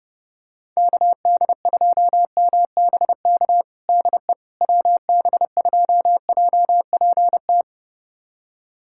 Morse code, heard as a keyed tone, about 32 wpm, 700 Hz. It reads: KD2MBK DE WB2JPT
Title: USA Callsign DE Callsign 30wpm